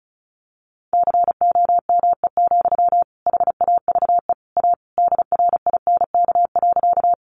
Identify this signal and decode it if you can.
Morse code, heard as a keyed tone, about 34 wpm, 700 Hz. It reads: COME, HAVE A DRINK.
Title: Sentences from Top 1500 Words 35wpm